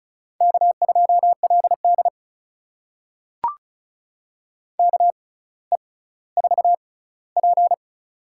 Morse code, heard as a keyed tone, about 36 wpm, spaced as for 18 wpm, 690 Hz. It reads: K2LD KE4P
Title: USA Call Signs - 4 Characters - 6x Character Spacing - 35wpm